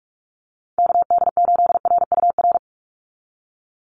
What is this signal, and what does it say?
Morse code, heard as a keyed tone, about 46 wpm, 700 Hz. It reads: KD8RUR
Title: USA Call Signs 45wpm